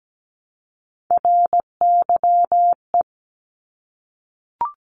Morse code, heard as a keyed tone, about 17 wpm, 700 Hz. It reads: RYE
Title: Multi-letter Phonemes Repeated 3x 17wpm